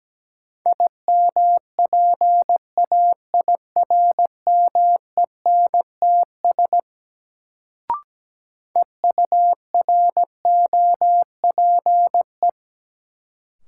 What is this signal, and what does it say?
Morse code, heard as a keyed tone, about 17 wpm, 700 Hz. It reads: IMPAIRMENTS EUROPE
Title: Sets of 5 Words Encore - New Word Drill - 17wpm